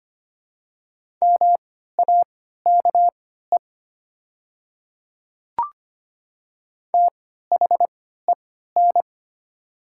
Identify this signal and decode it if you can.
Morse code, heard as a keyed tone, about 25 wpm, spaced as for 14 wpm, 700 Hz. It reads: MAKE THEN
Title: 3x Character Spacing - Top 100 Words - 25wpm